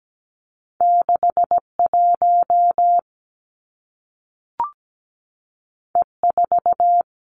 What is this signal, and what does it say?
Morse code, heard as a keyed tone, about 17 wpm, 700 Hz. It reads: 6 1 E 4
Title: Two Challenging Letter-Number Combinations 17wpm